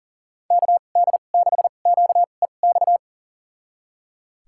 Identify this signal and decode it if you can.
Morse code, heard as a keyed tone, about 37 wpm, spaced as for 28 wpm, 700 Hz. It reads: KD6QEX
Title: USA Call Signs - 6 Characters - 2x Character Spacing - 40wpm